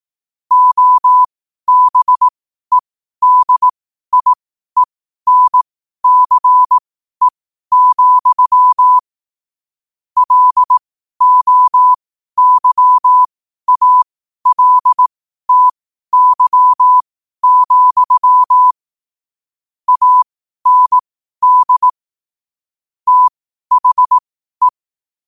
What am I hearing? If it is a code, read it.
Morse code, heard as a keyed tone, about 18 wpm, spaced as for 12 wpm, 1000 Hz. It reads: OBEDIENCE, LOYALTY, AND THE